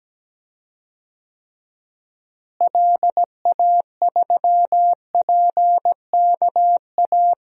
Morse code, heard as a keyed tone, about 17 wpm, 700 Hz. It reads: LA3PKA
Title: DXCC-100 Call Signs - 6 Characters - Repeated 2x - 17wpm